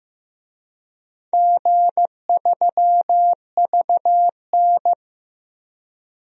Morse code, heard as a keyed tone, about 15 wpm, 700 Hz. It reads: G3VN
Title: DXCC-100 Call Signs - 4 Characters - Repeated 2x - 15wpm